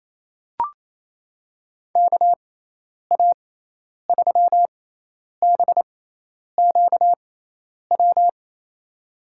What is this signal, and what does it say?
Morse code, heard as a keyed tone, about 28 wpm, spaced as for 10 wpm, 700 Hz. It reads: KA3BQW
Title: USA Call Signs - 6 Characters - 6x Character Spacing - 28wpm